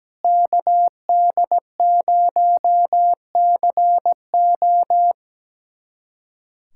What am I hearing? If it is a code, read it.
Morse code, heard as a keyed tone, about 17 wpm, 700 Hz. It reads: KD0CO
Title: USA Callsign DE Callsign - 5 Characters - 17wpm